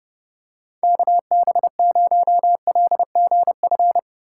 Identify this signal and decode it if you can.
Morse code, heard as a keyed tone, about 30 wpm, 700 Hz. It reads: KB0LGF
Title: K1USN SST QSO Practice - 30wpm